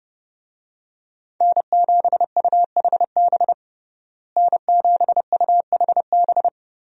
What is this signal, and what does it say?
Morse code, heard as a keyed tone, about 30 wpm, 700 Hz. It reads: N7UHB N7UHB
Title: USA Call Signs - 5 Characters - Repeated 3x - 30wpm